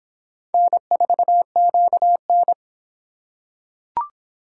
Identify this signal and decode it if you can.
Morse code, heard as a keyed tone, about 25 wpm, 700 Hz. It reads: N4QN